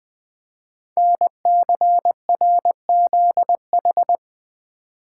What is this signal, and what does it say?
Morse code, heard as a keyed tone, about 20 wpm, 700 Hz. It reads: NCRZH
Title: Code Groups - 5 Letters - 20wpm